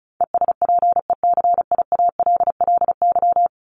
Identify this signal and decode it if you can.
Morse code, heard as a keyed tone, about 34 wpm, 700 Hz. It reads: ESPECIALLY
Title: Top 501-750 Words 35wpm